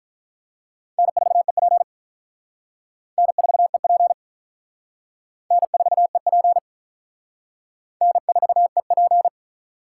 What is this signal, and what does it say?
Morse code, heard as a keyed tone, about 45 wpm, 700 Hz. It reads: N4EP N4EP N4EP N4EP